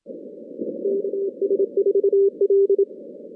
Morse code, heard as a keyed tone, about 27 wpm, 400 Hz. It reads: KS4L
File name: 2016 JAN CW NAQP / 2016 Jan naqp k5ot - 0-10